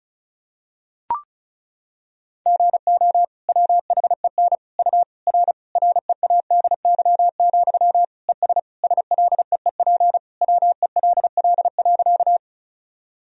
Morse code, heard as a keyed tone, about 36 wpm, 700 Hz. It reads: GO WHEN U R READY, ES SLEEP WELL.
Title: Sentences from Top 500 Words and Abbreviations 35wpm